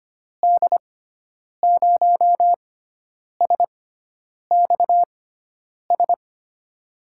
Morse code, heard as a keyed tone, about 25 wpm, spaced as for 9 wpm, 700 Hz. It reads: D0SXS
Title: USA Call Signs - 6 Characters - 6x Character Spacing - 25wpm